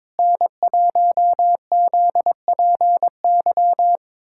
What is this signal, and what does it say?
Morse code, heard as a keyed tone, about 22 wpm, 700 Hz. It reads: N1ZPY
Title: USA Callsign DE Callsign - 5 Characters - 22wpm